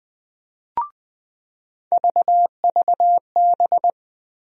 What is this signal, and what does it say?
Morse code, heard as a keyed tone, about 20 wpm, 700 Hz. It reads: VVB
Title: USA Call Sign Postfixes - 3 Letters 20wpm